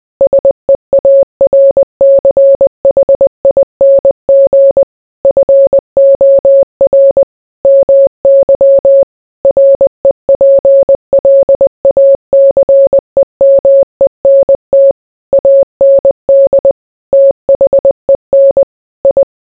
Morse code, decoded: SEARCHING FOR MY REPLACEMENT AND THEN I

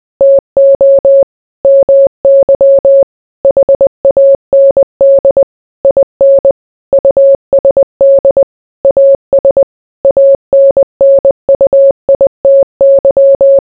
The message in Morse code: TO MY HAND IN USD AS ANNUITY

